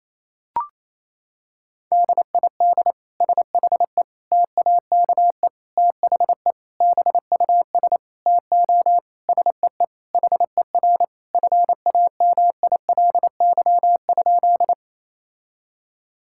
Morse code, heard as a keyed tone, about 28 wpm, 700 Hz. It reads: DID SHE TAKE THE BUS TO SEE HER FAMILY?
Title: Sentences from Top 1000 Words - Review Part 6 - 28wpm